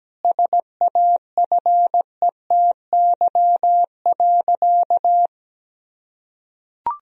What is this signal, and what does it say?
Morse code, heard as a keyed tone, about 17 wpm, 700 Hz. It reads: SAFETY.
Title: Sentences from Top 1000 Words - Review Part 4 - 17wpm